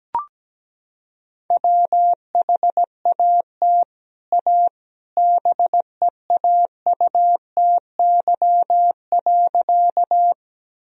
Morse code, heard as a keyed tone, about 17 wpm, 700 Hz. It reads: WHAT A BEAUTY.
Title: Sentences from Top 1000 Words - Review Part 2 - 17wpm